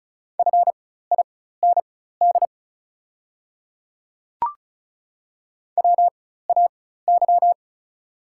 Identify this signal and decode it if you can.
Morse code, heard as a keyed tone, about 36 wpm, spaced as for 16 wpm, 700 Hz. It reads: FIND WAY
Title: 4x Character Spacing - Top 100 Words - 35wpm